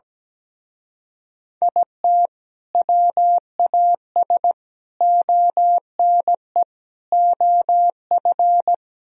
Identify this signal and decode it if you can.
Morse code, heard as a keyed tone, about 17 wpm, 700 Hz. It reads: IT WAS ONE OF
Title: Speed-Racing - Sets of 4 Words 17wpm